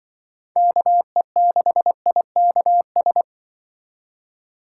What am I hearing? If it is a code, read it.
Morse code, heard as a keyed tone, about 27 wpm, 700 Hz. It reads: KE6IKS